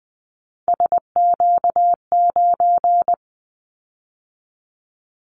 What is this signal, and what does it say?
Morse code, heard as a keyed tone, about 20 wpm, 700 Hz. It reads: SQ9
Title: Callsign Prefixes for Top 25 DXCC - 20wpm